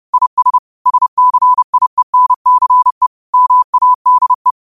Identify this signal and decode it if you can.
Morse code, heard as a keyed tone, about 30 wpm, 1000 Hz. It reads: IS SCIENCE MADE
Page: New quotes every day in morse code at 30 Words per minute.